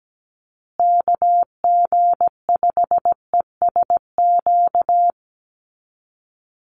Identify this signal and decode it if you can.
Morse code, heard as a keyed tone, about 17 wpm, 700 Hz. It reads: KG5ESQ
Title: USA Call Signs Repeated 3x 17wpm